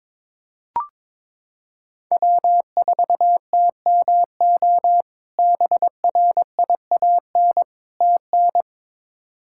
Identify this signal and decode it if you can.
Morse code, heard as a keyed tone, about 22 wpm, 700 Hz. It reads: W4TMO BRIAN TN
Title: K1USN SST QSO Practice - 22wpm